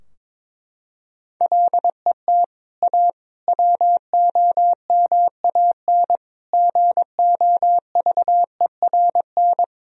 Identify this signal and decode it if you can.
Morse code, heard as a keyed tone, about 22 wpm, 700 Hz. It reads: LET A WOMAN GOVERN
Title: Sentences from Top 1000 Words - Review Part 2 - 22wpm